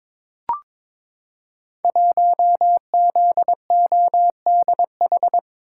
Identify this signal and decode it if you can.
Morse code, heard as a keyed tone, about 22 wpm, 700 Hz. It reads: 1ZODH
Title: Code Groups - 5 Letter:Numbers - 22wpm